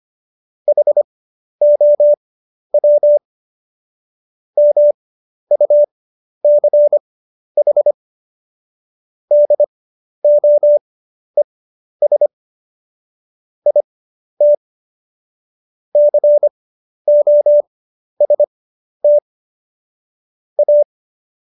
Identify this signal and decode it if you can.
Morse code, heard as a keyed tone, about 25 wpm, spaced as for 11 wpm, 590 Hz. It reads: HOW MUCH DOES IT COST A
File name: Jokes-25-char-10-eff